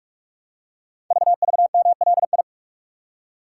Morse code, heard as a keyed tone, about 44 wpm, 700 Hz. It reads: VVMLI
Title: Code Groups - 5 Letters - 45wpm